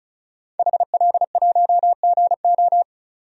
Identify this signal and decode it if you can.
Morse code, heard as a keyed tone, about 34 wpm, 700 Hz. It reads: HL1GO
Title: DXCC-100 Call Signs - 5 Characters - 35wpm